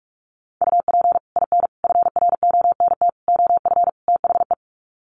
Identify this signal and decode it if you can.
Morse code, heard as a keyed tone, about 44 wpm, 700 Hz. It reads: UP IN FRONT OF THE
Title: Sets of 5 Words Encore2 45wpm